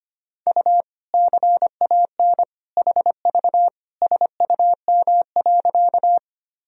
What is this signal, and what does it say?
Morse code, heard as a keyed tone, about 25 wpm, 700 Hz. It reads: U CAN HV SUM.
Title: Sentences from Top 100 Words and Abbreviations 25wpm